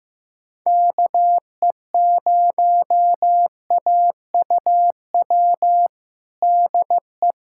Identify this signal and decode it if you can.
Morse code, heard as a keyed tone, about 15 wpm, 700 Hz. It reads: KE0AUW DE
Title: USA Callsign DE Callsign - 6 Characters - 15wpm